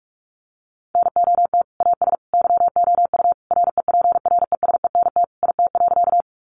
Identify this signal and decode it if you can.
Morse code, heard as a keyed tone, about 44 wpm, 700 Hz. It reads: NOT AS YOU REPRESENT IT.
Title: Sentences from Top 1200 Words 45wpm